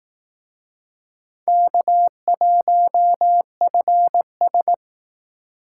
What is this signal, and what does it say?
Morse code, heard as a keyed tone, about 18 wpm, 700 Hz. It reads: K1FS
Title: Speed Racing - USA 1x2 Call Signs 15wpm